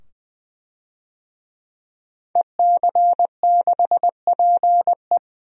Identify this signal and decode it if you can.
Morse code, heard as a keyed tone, about 20 wpm, 700 Hz. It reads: EC6PE
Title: DXCC-100 Call Signs - 5 Characters - 20wpm